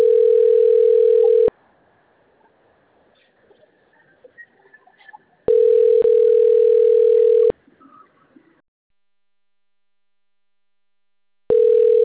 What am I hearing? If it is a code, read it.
Ringback tone (North America), 440 + 480 Hz.